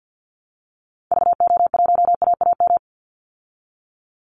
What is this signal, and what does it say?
Morse code, heard as a keyed tone, about 50 wpm, 700 Hz. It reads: VO1AAM
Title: Call Signs Repeated 3x - 50wpm